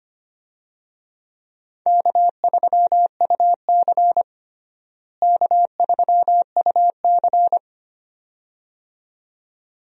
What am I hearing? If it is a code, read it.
Morse code, heard as a keyed tone, about 25 wpm, 700 Hz. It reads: K3UC K3UC
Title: USA Call Signs - 4 Characters - Repeated 2x - 25wpm